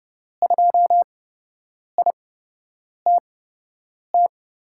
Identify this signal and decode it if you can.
Morse code, heard as a keyed tone, about 30 wpm, spaced as for 8 wpm, 700 Hz. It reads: 2ITT
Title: USA Call Signs - 5 Characters - 8x Character Spacing - 30wpm